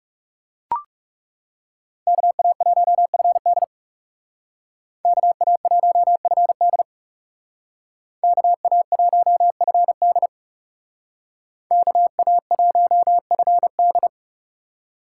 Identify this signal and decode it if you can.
Morse code, heard as a keyed tone, about 35 wpm, 700 Hz. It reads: KA1FD KA1FD KA1FD KA1FD
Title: Speed Racing - USA 2x2 Call Signs 30wpm